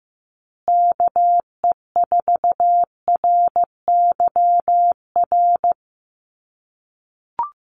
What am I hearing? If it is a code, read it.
Morse code, heard as a keyed tone, about 15 wpm, 700 Hz. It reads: KE4RYR
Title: USA Call Signs Repeated 3x 15wpm